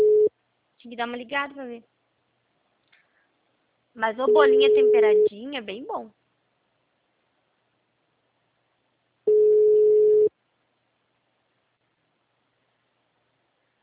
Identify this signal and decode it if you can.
Ringback tone (Europe, 425 Hz), with speech.